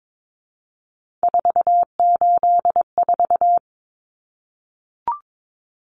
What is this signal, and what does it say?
Morse code, heard as a keyed tone, about 22 wpm, 700 Hz. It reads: 484
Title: Numbers - 3-Digit 22wpm